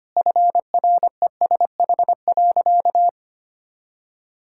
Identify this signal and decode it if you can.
Morse code, heard as a keyed tone, about 25 wpm, 700 Hz. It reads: FRESH.
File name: practice / 25wpm / Sentences with 2-Word Phrases - 25wpm